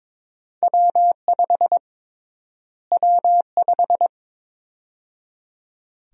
Morse code, heard as a keyed tone, about 22 wpm, 700 Hz. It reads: W5 W5
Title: USA Call Sign Prefixes - Repeated 2x - 22wpm